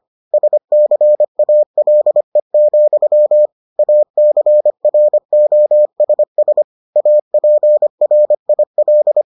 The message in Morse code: SCALE, ACROSS APRIL

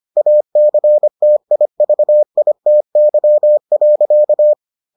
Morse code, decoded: ACTIVITY.

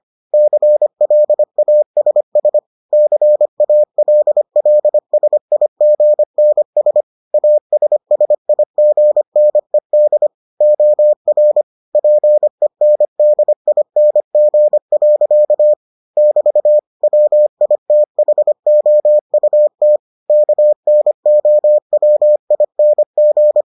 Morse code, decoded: CLASS CALLSIGNS ASSIGNED OR PENDING. <BT> WITHOUT KNOWING

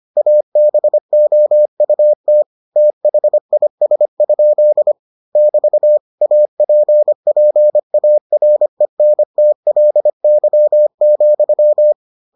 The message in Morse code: ABOUT THIS? <BT> APPARENTLY,

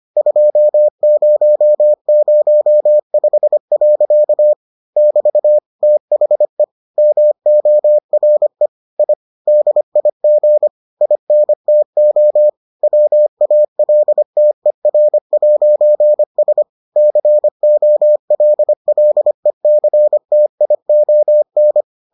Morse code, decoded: 2005. <BT> THE MORE I DIG INTO WALTER'S COLLECTION

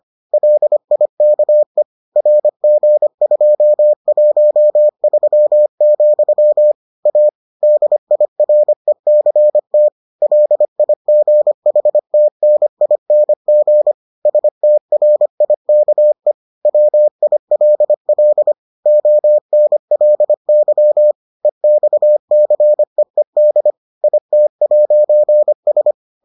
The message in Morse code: LIKE RG213, A DIRECT LIGHTNING STRIKE WILL ONLY EXCEED IT'S

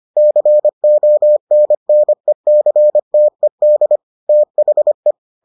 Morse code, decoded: CONNECTED THE

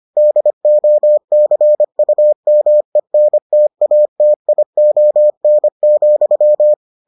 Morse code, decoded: DOCUMENTATION,

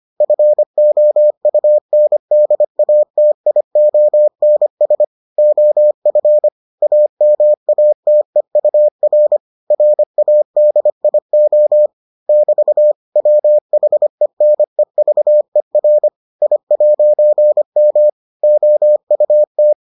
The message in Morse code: FOUNDATIONS OF AMATEUR RADIO <BT> WHENEVER I'M OUT

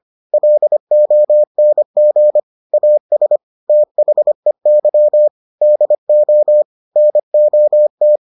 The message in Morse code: LONG AS THEY DO NOT

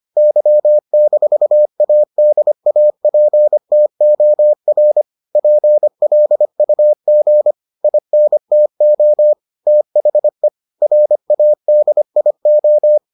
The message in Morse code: Y-ADAPTOR PLUG INTO THE RADIO

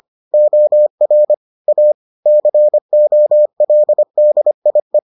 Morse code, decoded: OR A COLDIE

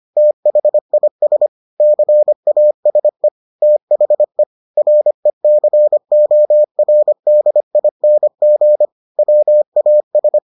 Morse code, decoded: THIS CASE THE RECORDING WAS

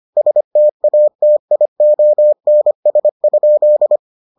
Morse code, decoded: STATIONS?

